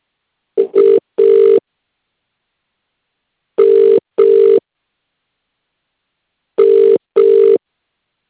Ringback tone (UK), 400 + 450 Hz.